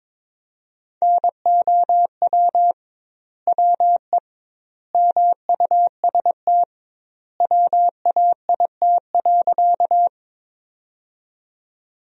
Morse code, heard as a keyed tone, about 22 wpm, 700 Hz. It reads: NOW WE MUST WAIT.
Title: 2x Word-Spacing - Sentences from Top 500 Words - 22wpm